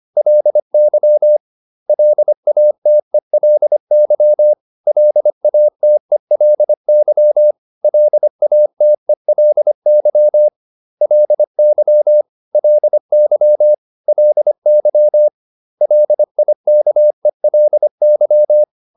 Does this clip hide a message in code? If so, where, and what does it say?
Yes: Morse code from 0.2 to 18.6 s, heard as a keyed tone, about 25 wpm, 600 Hz. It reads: LY LATELY LATELY LATELY LY LY LY LIKELY